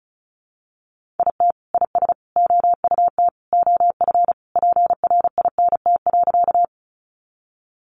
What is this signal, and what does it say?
Morse code, heard as a keyed tone, about 34 wpm, 700 Hz. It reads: IT IS OUT OF PRINT.
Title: Sentences from Top 1400 Words 35wpm